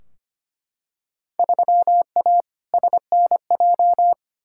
Morse code, heard as a keyed tone, about 25 wpm, 700 Hz. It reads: 3A SNJ
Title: Field Day Exchanges - 25wpm